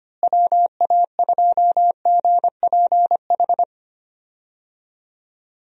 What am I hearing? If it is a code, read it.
Morse code, heard as a keyed tone, about 25 wpm, 700 Hz. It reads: WA2GPH
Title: USA Call Signs - 6 Characters - Repeated 3x - 25wpm